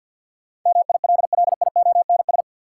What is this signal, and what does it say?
Morse code, heard as a keyed tone, about 54 wpm, 700 Hz. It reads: MILLIONS
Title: Sets of 5 Words - New Word Drill - 50wpm